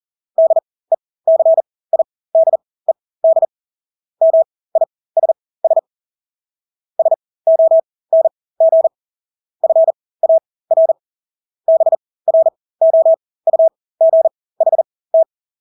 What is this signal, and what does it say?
Morse code, heard as a keyed tone, about 40 wpm, spaced as for 20 wpm, 650 Hz. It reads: DECIDED MISS SONG FAR BROUGHT
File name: Engelske ord 40-20 wpm